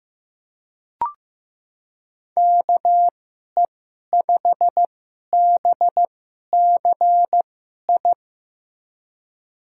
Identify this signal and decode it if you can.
Morse code, heard as a keyed tone, about 15 wpm, spaced as for 11 wpm, 700 Hz. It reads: KE5BCI
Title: USA Call Signs - 6 Characters - 2x Character Spacing - 15wpm